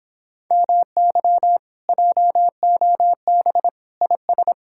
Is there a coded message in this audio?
Yes: Morse code from 0.5 to 4.5 s, heard as a keyed tone, about 29 wpm, 700 Hz. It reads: MY JOB IS